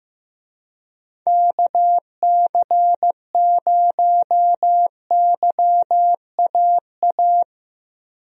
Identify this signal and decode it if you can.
Morse code, heard as a keyed tone, about 15 wpm, 700 Hz. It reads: KC0YAA
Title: USA Call Signs - 6 Characters - Repeated 3x - 15wpm